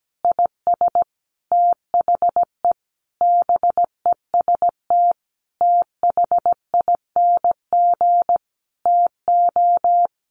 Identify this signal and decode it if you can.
Morse code, heard as a keyed tone, about 17 wpm, 700 Hz. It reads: IS THE BEST THING TO
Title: Sentences from Top 1200 Words 17wpm